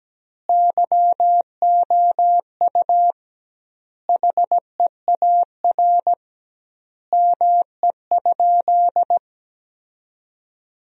Morse code, heard as a keyed tone, about 17 wpm, 700 Hz. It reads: YOU HEAR ME?
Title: 2x Word-Spacing - Sentences from Top 300 Words - 17wpm